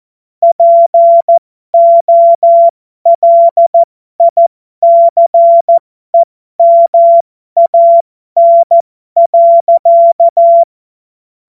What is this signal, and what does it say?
Morse code, heard as a keyed tone, about 14 wpm, spaced as for 12 wpm, 680 Hz. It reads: POLICEMAN.
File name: Morsekurs / Witze